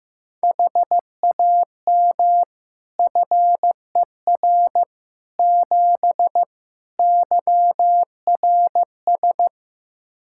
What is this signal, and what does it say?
Morse code, heard as a keyed tone, about 15 wpm, 700 Hz. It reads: HAM FER 7 YRS
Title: Practice QSO - Exchange 3 - More Info - 15wpm